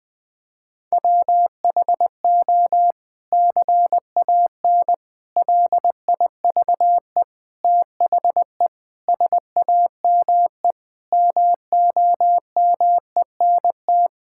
Morse code, heard as a keyed tone, about 20 wpm, 700 Hz. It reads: WHO CAN LIVE THE SAME MOMENT
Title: Sentences from Top 1600 Words 20wpm